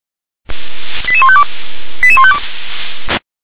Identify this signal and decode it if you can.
Five-tone selective call (ZVEI), heard as quick runs of five tones, tones of about 80 ms. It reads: 88152 88152